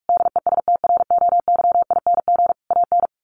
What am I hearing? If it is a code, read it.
Morse code, heard as a keyed tone, about 44 wpm, 700 Hz. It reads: DESTROYING AN
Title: A Sound Of Thunder 45wpm